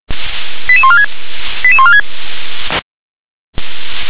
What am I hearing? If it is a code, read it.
Five-tone selective call (ZVEI), heard as quick runs of five tones, tones of about 70 ms. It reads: 88146 88146
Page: /357/SDR/ZVEI/